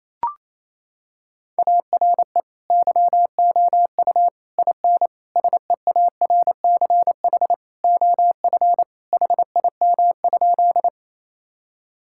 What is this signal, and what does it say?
Morse code, heard as a keyed tone, about 27 wpm, 700 Hz. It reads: ARE YOU IN SEARCH OF HIM?
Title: Sentences from Top 1000 Words - Review Part 5 - 28wpm